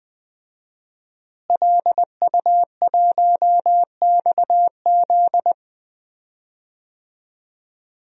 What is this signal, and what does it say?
Morse code, heard as a keyed tone, about 20 wpm, 700 Hz. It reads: LU1XZ
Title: DXCC-100 Call Signs - 5 Characters - Repeated 2x - 20wpm